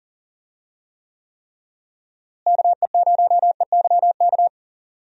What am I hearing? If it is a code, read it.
Morse code, heard as a keyed tone, about 40 wpm, 700 Hz. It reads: KE0EYK
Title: USA Call Signs - 6 Characters - Repeated 3x - 40wpm